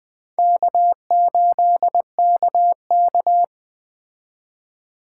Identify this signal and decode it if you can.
Morse code, heard as a keyed tone, about 20 wpm, 700 Hz. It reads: K8KK